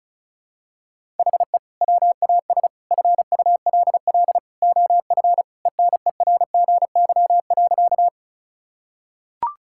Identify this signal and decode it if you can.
Morse code, heard as a keyed tone, about 36 wpm, 700 Hz. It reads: HE WAS FULL OF ENERGY.
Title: Sentences from Top 1000 Words - Review Part 6 - 35wpm